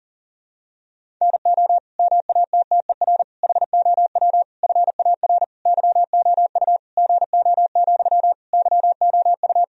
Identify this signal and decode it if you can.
Morse code, heard as a keyed tone, about 40 wpm, 700 Hz. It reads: NO MATTER HOW FAR YOU GO, YOU
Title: Sentences from Top 1000 Words - Review Part 1 - 40wpm